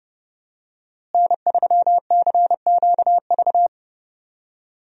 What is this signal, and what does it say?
Morse code, heard as a keyed tone, about 30 wpm, 700 Hz. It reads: N3CQV
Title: USA Call Signs - 5 Characters - Repeated 2x - 30wpm